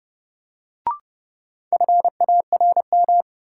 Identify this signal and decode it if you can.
Morse code, heard as a keyed tone, about 31 wpm, 700 Hz. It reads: FARM
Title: Sentences from Top 1000 Words - Review Part 2 - New Word Drill - 30wpm